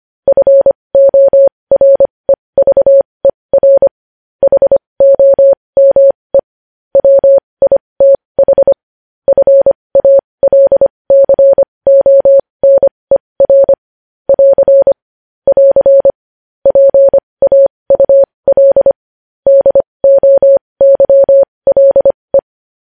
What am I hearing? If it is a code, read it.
Morse code, heard as a keyed tone, about 25 wpm, spaced as for 20 wpm, 550 Hz. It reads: FOREVER HOME WITH FALCONER <AR> <AR> PAUL DOYLE
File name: podcast / News headlines-1x-25x20